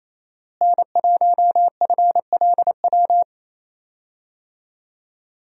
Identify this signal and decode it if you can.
Morse code, heard as a keyed tone, about 27 wpm, 700 Hz. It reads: N1FLW
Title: USA Call Signs - 5 Characters - Repeated 3x - 28wpm